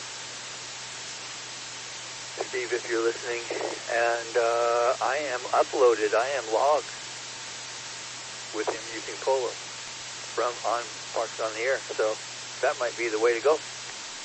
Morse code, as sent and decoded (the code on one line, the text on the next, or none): none